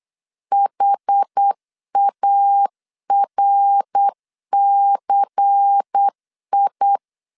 Morse code, heard as a keyed tone, about 8 wpm, 800 Hz. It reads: HARCI
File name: LEZIONI / 6 PKX0 / 6 PKX 8wpm 3041